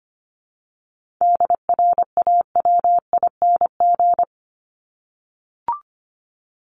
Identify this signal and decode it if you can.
Morse code, heard as a keyed tone, about 25 wpm, 700 Hz. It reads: DRAWING
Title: ING Suffix - Top 100 Words with Base Words 25wpm